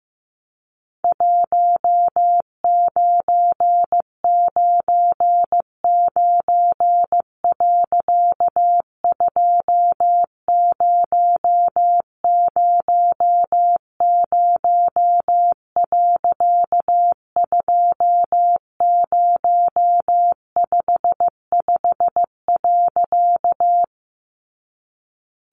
Morse code, heard as a keyed tone, about 15 wpm, 700 Hz. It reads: 1999. 2000. 2055.
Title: A Sound Of Thunder 15wpm